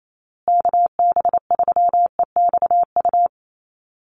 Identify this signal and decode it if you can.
Morse code, heard as a keyed tone, about 28 wpm, 700 Hz. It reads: KB3EXU
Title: USA Call Signs Repeated 2x 28wpm